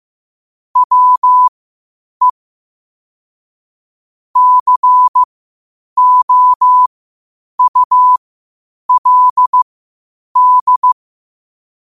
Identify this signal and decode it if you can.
Morse code, heard as a keyed tone, about 15 wpm, spaced as for 8 wpm, 1000 Hz. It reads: WE COULD
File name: qotdfiles / 8wpm